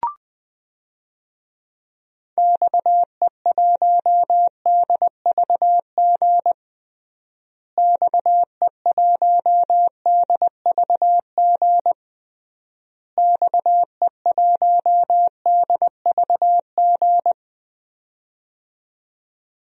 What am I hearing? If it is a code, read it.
Morse code, heard as a keyed tone, about 20 wpm, 700 Hz. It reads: XE1DVG XE1DVG XE1DVG
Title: DXCC-100 Call Signs - 6 Characters - Repeated 3x - 20wpm